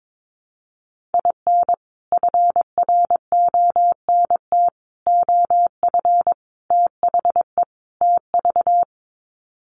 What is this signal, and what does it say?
Morse code, heard as a keyed tone, about 22 wpm, 700 Hz. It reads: IN FRONT OF THE TV
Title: Sets of 5 Words Encore2 22wpm